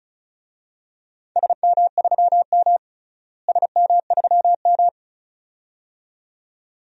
Morse code, heard as a keyed tone, about 36 wpm, 700 Hz. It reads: SM3M SM3M
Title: DXCC-100 Call Signs - 4 Characters - Repeated 2x - 35wpm